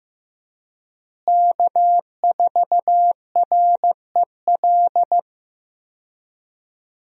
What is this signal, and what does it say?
Morse code, heard as a keyed tone, about 15 wpm, 700 Hz. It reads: K4REL
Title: USA Call Signs - 5 Characters - Repeated 3x - 15wpm